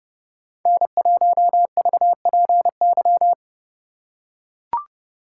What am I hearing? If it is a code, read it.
Morse code, heard as a keyed tone, about 28 wpm, 700 Hz. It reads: N1VPY